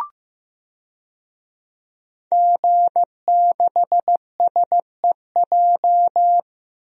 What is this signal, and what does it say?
Morse code, heard as a keyed tone, about 15 wpm, 700 Hz. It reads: G6SEJ
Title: DXCC-100 Call Signs - 5 Characters - Repeated 2x - 15wpm